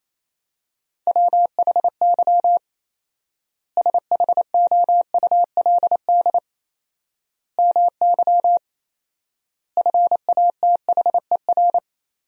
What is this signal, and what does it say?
Morse code, heard as a keyed tone, about 28 wpm, 700 Hz. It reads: WHY SHOULD MY FATHER
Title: 4x Word-Spacing - Sentences from Top 500 Words - 28wpm